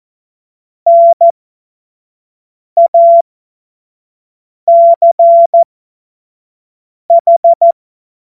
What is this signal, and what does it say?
Morse code, heard as a keyed tone, about 14 wpm, spaced as for 5 wpm, 680 Hz. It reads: NACH
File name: Morsekurs / Witze / Witz 02. Februar 2021, 25 F70